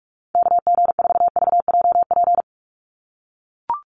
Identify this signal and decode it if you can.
Morse code, heard as a keyed tone, about 44 wpm, 700 Hz. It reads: KG4VJP
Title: USA Call Signs 45wpm